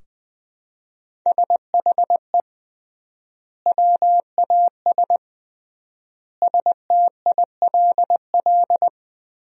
Morse code, heard as a keyed tone, about 20 wpm, 700 Hz. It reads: SHE WAS STILL
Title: 3x Word-Spacing - Sentences from Top 500 Words - 20wpm